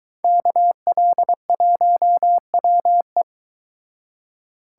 Morse code, heard as a keyed tone, about 22 wpm, 700 Hz. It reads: KL1WE